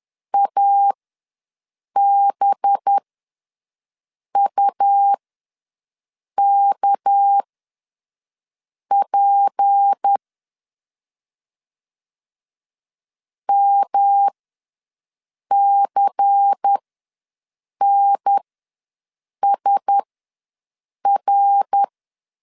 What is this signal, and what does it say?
Morse code, heard as a keyed tone, about 10 wpm, spaced as for 5 wpm, 800 Hz. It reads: ABUKP MCNSR
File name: LEZIONI / 7 GF9Z / 7 GF9 ws 10wpm 4034